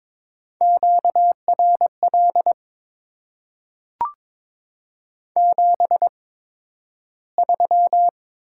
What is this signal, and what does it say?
Morse code, heard as a keyed tone, about 22 wpm, spaced as for 12 wpm, 700 Hz. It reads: QRL 73
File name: Top 20 QSO Elements - 8x Character Spacing - 22wpm